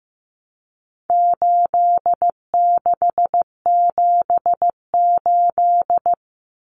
Morse code, heard as a keyed tone, about 15 wpm, 700 Hz. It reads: 8678
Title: Numbers - 4-Digit 15wpm